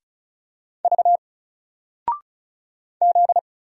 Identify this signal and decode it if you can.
Morse code, heard as a keyed tone, about 36 wpm, 700 Hz. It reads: V Z
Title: Single Character (Letter:Number:Punctuation:Prosign) 35wpm